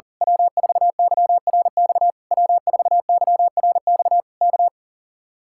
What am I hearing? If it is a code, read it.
Morse code, heard as a keyed tone, about 38 wpm, 700 Hz. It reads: W4YRX W4YRX K
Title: Speed-Racing Common QSO Phrases - 40wpm